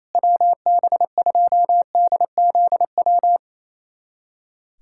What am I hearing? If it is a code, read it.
Morse code, heard as a keyed tone, about 28 wpm, 700 Hz. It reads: WB2DZW
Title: USA Callsign DE Callsign - 6 Characters - 28wpm